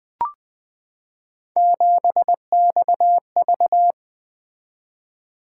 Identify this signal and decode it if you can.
Morse code, heard as a keyed tone, about 20 wpm, 700 Hz. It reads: 7XV
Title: Three Letter-Number Combinations - 20wpm